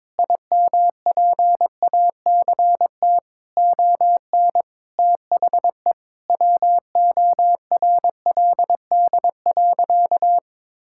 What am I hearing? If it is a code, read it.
Morse code, heard as a keyed tone, about 22 wpm, 700 Hz. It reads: IMPACT ON THE WORLD.
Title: Sentences from Top 1000 Words - Review Part 6 - 22wpm